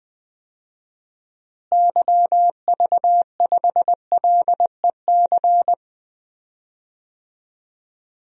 Morse code, heard as a keyed tone, about 20 wpm, 700 Hz. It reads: YV5LEC
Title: DXCC-100 Call Signs - 6 Characters - Repeated 3x - 20wpm